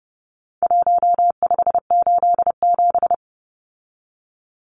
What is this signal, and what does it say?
Morse code, heard as a keyed tone, about 30 wpm, 700 Hz. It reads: 1587
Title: Numbers - 4-Digit 30wpm